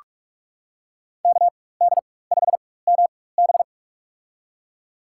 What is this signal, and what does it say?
Morse code, heard as a keyed tone, about 42 wpm, spaced as for 20 wpm, 700 Hz. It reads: KD5MB
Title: USA Call Signs - 4x Character Spacing - 45wpm